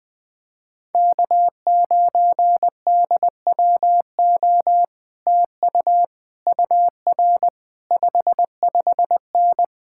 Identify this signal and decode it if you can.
Morse code, heard as a keyed tone, about 20 wpm, 700 Hz. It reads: K9DWO TU UR 55N
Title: Speed-Racing Common QSO Phrases - 17wpm